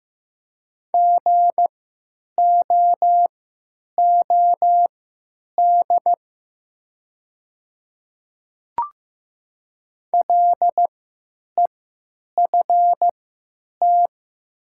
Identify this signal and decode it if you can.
Morse code, heard as a keyed tone, about 15 wpm, spaced as for 8 wpm, 700 Hz. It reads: GOOD LEFT
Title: 3x Character Spacing - Top 300 Words - 15wpm